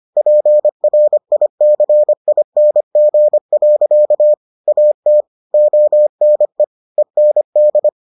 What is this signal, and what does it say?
Morse code, heard as a keyed tone, about 25 wpm, 600 Hz. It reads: PRICING. AT ONE END